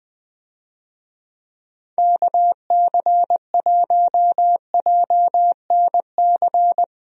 Morse code, heard as a keyed tone, about 20 wpm, 700 Hz. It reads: KC1JNC
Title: USA Call Signs - 6 Characters - Repeated 2x - 20wpm